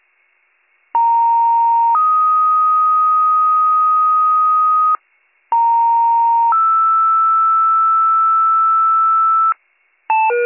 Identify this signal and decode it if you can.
Two-tone page, heard as a one-second tone, then a three-second tone, about 930 then 1250 Hz, about 930 then 1320 Hz. It is heard from 0.9 to 9.5 s.